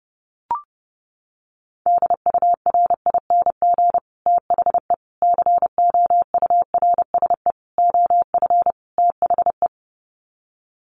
Morse code, heard as a keyed tone, about 30 wpm, 700 Hz. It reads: DURING THE COURSE OF THE
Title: Sets of 5 Words Encore 30wpm